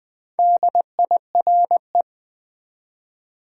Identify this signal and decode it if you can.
Morse code, heard as a keyed tone, about 20 wpm, 700 Hz. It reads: DIRE
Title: Build A Word - 2 Letters At A Time - 8-Letter Words - 20wpm